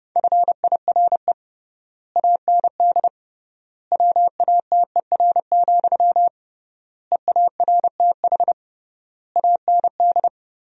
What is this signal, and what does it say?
Morse code, heard as a keyed tone, about 32 wpm, 700 Hz. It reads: FIRE AND WATER, EARTH AND
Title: 3x Word-Spacing - Sentences from Top 400 Words - 30wpm